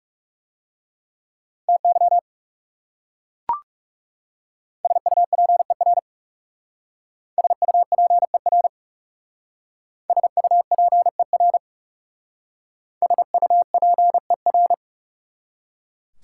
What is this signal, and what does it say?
Morse code, heard as a keyed tone, about 37 wpm, 700 Hz. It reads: TY SUPER SUPER SUPER SUPER
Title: Speed-Racing Prefixes and Suffixes 30wpm